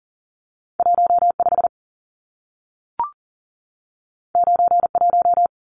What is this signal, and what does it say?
Morse code, heard as a keyed tone, about 40 wpm, 700 Hz. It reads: 15 91
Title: Numbers - 2-Digit 40wpm